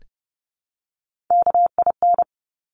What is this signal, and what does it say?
Morse code, heard as a keyed tone, about 30 wpm, 700 Hz. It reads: KIN
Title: Top Three Letter Combinations in Words 30wpm